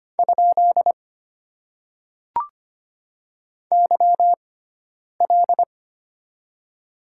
Morse code, heard as a keyed tone, about 25 wpm, spaced as for 9 wpm, 700 Hz. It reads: ? YL
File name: Top 90 QSO Elements - 6x Character Spacing - 25wpm